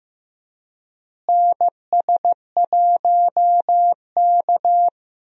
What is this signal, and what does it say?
Morse code, heard as a keyed tone, about 16 wpm, 700 Hz. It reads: NS1K
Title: USA Call Signs - 4 Characters - Repeated 3x - 15wpm